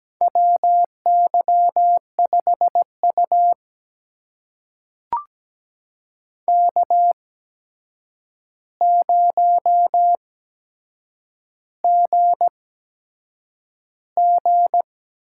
Morse code, heard as a keyed tone, about 17 wpm, spaced as for 7 wpm, 700 Hz. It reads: WY5U K0GG
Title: USA Call Signs - 4 Characters - 8x Character Spacing - 17wpm